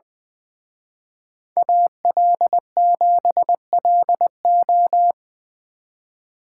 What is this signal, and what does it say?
Morse code, heard as a keyed tone, about 24 wpm, 700 Hz. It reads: AL7LO
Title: Speed Racing - USA 2x2 Call Signs 20wpm